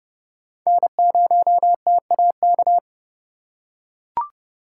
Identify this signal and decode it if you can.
Morse code, heard as a keyed tone, about 29 wpm, 700 Hz. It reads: N0TAK